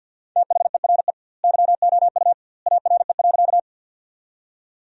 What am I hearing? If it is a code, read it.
Morse code, heard as a keyed tone, about 50 wpm, 700 Hz. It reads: THERE YOU ARE.
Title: 2x Word-Spacing - Sentences from Top 100 Words - 50wpm